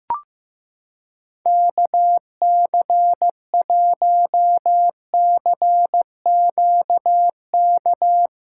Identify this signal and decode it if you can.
Morse code, heard as a keyed tone, about 15 wpm, 700 Hz. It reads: KC1CQK
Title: USA Callsign DE Callsign 15wpm